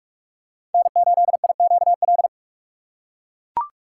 Morse code, heard as a keyed tone, about 41 wpm, 700 Hz. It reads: N8IQL
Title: USA Call Signs - 3x Character Spacing - 45wpm